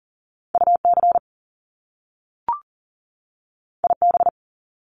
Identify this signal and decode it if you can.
Morse code, heard as a keyed tone, about 38 wpm, 700 Hz. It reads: UC IB
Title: Top Two Letter Combinations in Words 40wpm